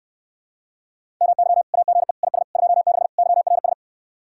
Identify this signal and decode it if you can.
Morse code, heard as a keyed tone, about 69 wpm, 700 Hz. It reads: MY AGE IS 26 YRS